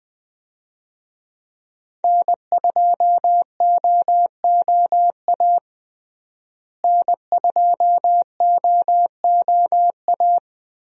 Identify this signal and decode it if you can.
Morse code, heard as a keyed tone, about 21 wpm, 700 Hz. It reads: N2OOA N2OOA
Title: USA Call Signs - 5 Characters - Repeated 2x - 20wpm